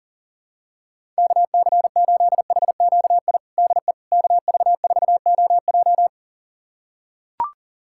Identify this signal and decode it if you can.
Morse code, heard as a keyed tone, about 40 wpm, 700 Hz. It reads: KC8HQI DE KV4OJ
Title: USA Callsign DE Callsign 40wpm